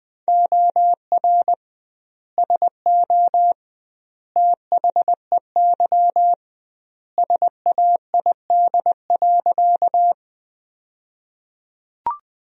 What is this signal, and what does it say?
Morse code, heard as a keyed tone, about 20 wpm, 700 Hz. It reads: OR SO THEY SAID.
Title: 2x Word-Spacing - Sentences from Top 100 Words - 20wpm